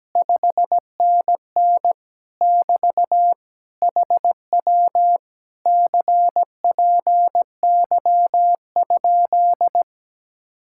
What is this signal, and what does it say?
Morse code, heard as a keyed tone, about 17 wpm, 700 Hz. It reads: 5NN <BT> HW CPY?
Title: Speed-Racing Common QSO Phrases - 17wpm